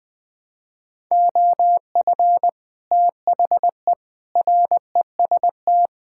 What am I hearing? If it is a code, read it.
Morse code, heard as a keyed tone, about 20 wpm, 700 Hz. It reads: OF THE REST
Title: Build A Phrase - 5-Word Phrases - 20wpm